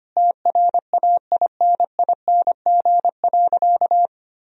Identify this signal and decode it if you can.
Morse code, heard as a keyed tone, about 25 wpm, 700 Hz. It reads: TRAINING.
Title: Sentences from Top 1000 Words - Review Part 4 - 25wpm